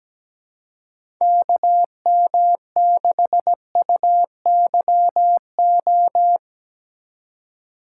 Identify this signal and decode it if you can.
Morse code, heard as a keyed tone, about 17 wpm, 700 Hz. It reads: KM6UYO
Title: USA Call Signs - 2x Character Spacing - 17wpm